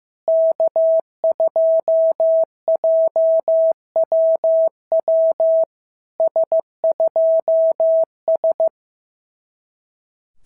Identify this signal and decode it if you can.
Morse code, heard as a keyed tone, about 15 wpm, 650 Hz. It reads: K2JWW S2S
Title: SOTA QSO Practice 15wpm